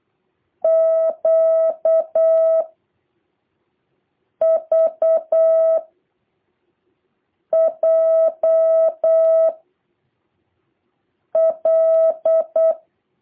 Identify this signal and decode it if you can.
Morse code, heard as a keyed tone, about 8 wpm, 650 Hz. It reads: Q V J L
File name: jh8cbh / 2023cw